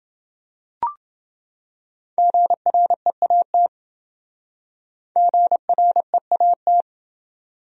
Morse code, heard as a keyed tone, about 26 wpm, 700 Hz. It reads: GREAT GREAT